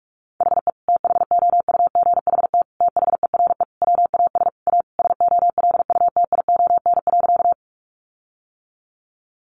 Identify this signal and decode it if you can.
Morse code, heard as a keyed tone, about 44 wpm, 700 Hz. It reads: HE THOUGHT THERE WAS A SOLUTION.
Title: Sentences from Top 1300 Words 45wpm